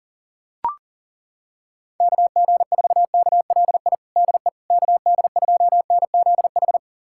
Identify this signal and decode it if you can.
Morse code, heard as a keyed tone, about 40 wpm, 700 Hz. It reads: KG4KLI DE KD2NZH
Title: USA Callsign DE Callsign 40wpm